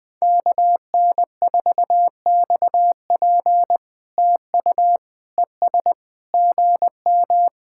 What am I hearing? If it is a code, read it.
Morse code, heard as a keyed tone, about 21 wpm, 700 Hz. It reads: KN4XP TU ES GM
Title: POTA QSO Practice - 20wpm